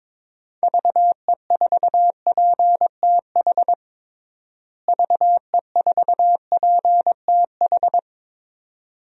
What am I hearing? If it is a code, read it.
Morse code, heard as a keyed tone, about 22 wpm, 700 Hz. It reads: VE4PTH VE4PTH
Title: DXCC-100 Call Signs - 6 Characters - Repeated 2x - 22wpm